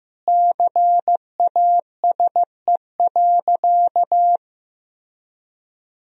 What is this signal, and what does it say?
Morse code, heard as a keyed tone, about 15 wpm, 700 Hz. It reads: CASE.
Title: Sentences from Top 1000 Words - Review Part 2 - 15wpm